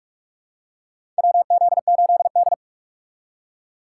Morse code, heard as a keyed tone, about 47 wpm, 700 Hz. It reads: WZ8D
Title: USA Call Signs - 4 Characters - 3x Character Spacing - 45wpm